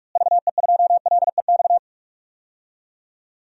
Morse code, heard as a keyed tone, about 46 wpm, 700 Hz. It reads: VE2LEX
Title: DXCC-100 Call Signs - 6 Characters - Repeated 2x - 45wpm